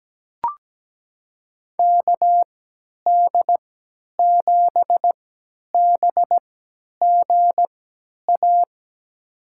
Morse code, heard as a keyed tone, about 17 wpm, spaced as for 10 wpm, 700 Hz. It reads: KD7BGA
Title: USA Call Signs - 6 Characters - 3x Character Spacing - 17wpm